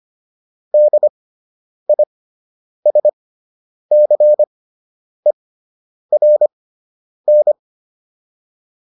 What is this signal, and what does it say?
Morse code, heard as a keyed tone, about 25 wpm, spaced as for 9 wpm, 600 Hz. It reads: DISCERN